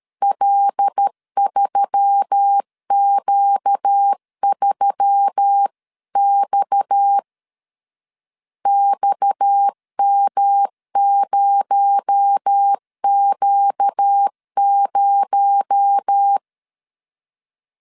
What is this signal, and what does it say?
Morse code, heard as a keyed tone, about 13 wpm, 800 Hz. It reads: L3Q3X XM0Q0
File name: CW / DET / 1 6 1500C S B 12wpm 3431